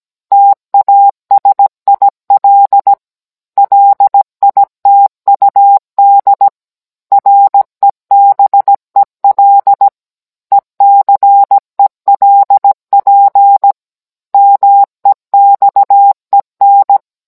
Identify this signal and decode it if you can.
Morse code, heard as a keyed tone, about 17 wpm, 800 Hz. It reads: TASIL LITUD REBEL ECELP MEXEN